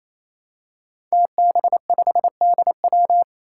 Morse code, heard as a keyed tone, about 27 wpm, 700 Hz. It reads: TB5DW
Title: Five Challenging Letter-Number Combinations 28wpm